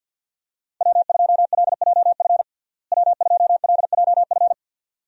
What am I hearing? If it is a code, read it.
Morse code, heard as a keyed tone, about 48 wpm, 700 Hz. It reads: W2LJF W2LJF
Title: USA Call Signs - 5 Characters - Repeated 2x - 50wpm